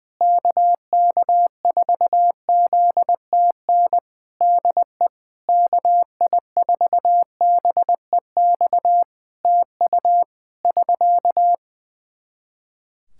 Morse code, heard as a keyed tone, about 21 wpm, 700 Hz. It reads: KK4ZTN DE KI4BEX TU <SK>
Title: K1USN SST QSO Practice 20wpm